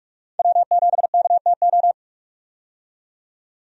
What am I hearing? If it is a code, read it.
Morse code, heard as a keyed tone, about 46 wpm, 700 Hz. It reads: W7KTO
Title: USA Call Signs - 5 Characters - 45wpm